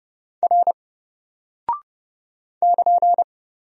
Morse code, heard as a keyed tone, about 30 wpm, 700 Hz. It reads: R <KN>
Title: Single Character (Letter:Number:Punctuation:Prosign) 30wpm